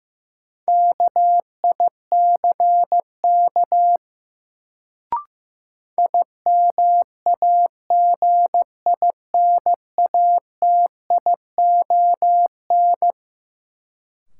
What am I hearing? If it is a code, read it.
Morse code, heard as a keyed tone, about 15 wpm, 700 Hz. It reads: KICK IMAGINATION
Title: Sentences from Top 1300 Words - New Word Drill - 15wpm